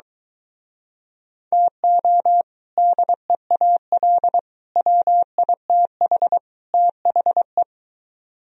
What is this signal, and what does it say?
Morse code, heard as a keyed tone, about 24 wpm, 700 Hz. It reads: TO DEAL WITH THE